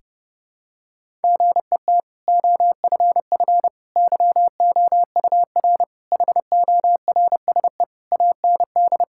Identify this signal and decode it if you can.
Morse code, heard as a keyed tone, about 30 wpm, 700 Hz. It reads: GET OFF YOUR HORSE AND
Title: Sentences from Top 1600 Words 30wpm